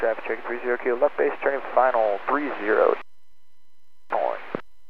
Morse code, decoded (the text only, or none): none